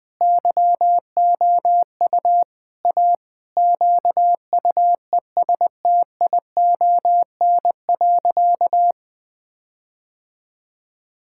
Morse code, heard as a keyed tone, about 20 wpm, 700 Hz. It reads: YOU A QUESTION.
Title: Sentences with 5-Word Phrases - 20wpm